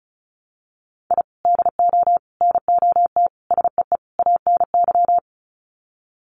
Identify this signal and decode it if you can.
Morse code, heard as a keyed tone, about 34 wpm, 700 Hz. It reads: I DO NOT SEE ANY
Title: Sets of 5 Words 35wpm